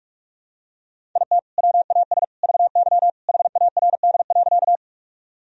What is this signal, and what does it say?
Morse code, heard as a keyed tone, about 46 wpm, 700 Hz. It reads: IT WAS VY HARD.
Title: Sentences from Top 300 Words and Abbreviations 45wpm